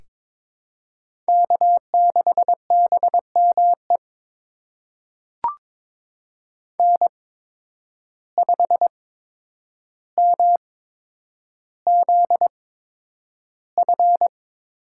Morse code, heard as a keyed tone, about 22 wpm, spaced as for 9 wpm, 700 Hz. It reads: K6BME N5MZF
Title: USA Call Signs - 5 Characters - 8x Character Spacing - 22wpm